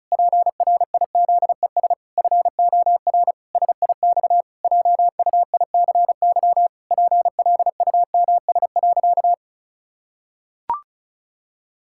Morse code, heard as a keyed tone, about 36 wpm, 700 Hz. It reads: PRIZES FOR SIX JUICY PLUMS.
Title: Pangram Challenge v2 - 35wpm